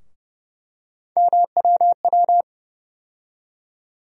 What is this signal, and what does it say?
Morse code, heard as a keyed tone, about 30 wpm, 700 Hz. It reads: MWW
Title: USA Call Sign Postfixes - 3 Letters 30wpm